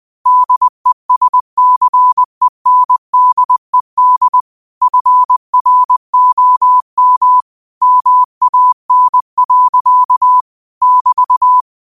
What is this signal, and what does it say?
Morse code, heard as a keyed tone, about 20 wpm, 1000 Hz. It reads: DESCENDED FROM MAN. <BT>
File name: qotdfiles / 20wpm